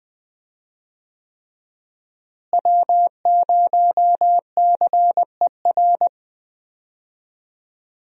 Morse code, heard as a keyed tone, about 20 wpm, 700 Hz. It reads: W0CER
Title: USA Call Signs - 5 Characters - Repeated 3x - 20wpm